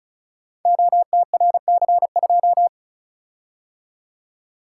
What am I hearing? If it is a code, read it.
Morse code, heard as a keyed tone, about 36 wpm, 700 Hz. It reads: OTRC2
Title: Code Groups - 5 Letter:Numbers - 35wpm